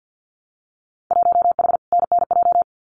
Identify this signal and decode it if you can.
Morse code, heard as a keyed tone, about 50 wpm, 700 Hz. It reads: 1H NNJ
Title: Winter Field Day Exchanges 50wpm